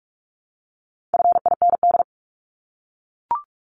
Morse code, heard as a keyed tone, about 44 wpm, 700 Hz. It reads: FIND
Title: ING Suffix - Top 100 Words with Base Words 45wpm